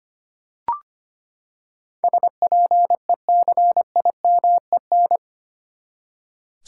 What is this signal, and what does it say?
Morse code, heard as a keyed tone, about 25 wpm, 700 Hz. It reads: SPECIMEN
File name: Eight Letter Words - 25wpm